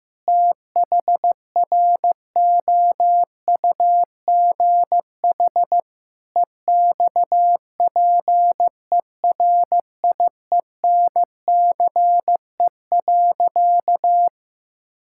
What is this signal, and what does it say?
Morse code, heard as a keyed tone, about 15 wpm, 700 Hz. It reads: THROUGH EXPERIENCE.
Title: Sentences from Top 1000 Words - Review Part 5 - 15wpm